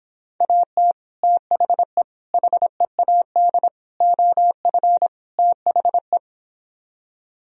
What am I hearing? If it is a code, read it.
Morse code, heard as a keyed tone, about 26 wpm, 700 Hz. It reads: AT THE HEAD OF THE